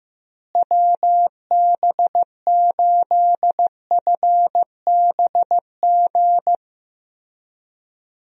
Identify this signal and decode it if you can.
Morse code, heard as a keyed tone, about 15 wpm, 700 Hz. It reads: WB8FBG
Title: USA Call Signs - 6 Characters - Repeated 2x - 15wpm